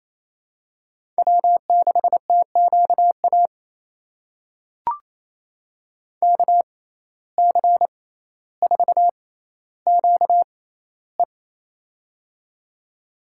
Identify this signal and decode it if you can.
Morse code, heard as a keyed tone, about 27 wpm, spaced as for 14 wpm, 700 Hz. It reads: W6TQA KC4QE
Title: USA Call Signs - 5 Characters - 6x Character Spacing - 28wpm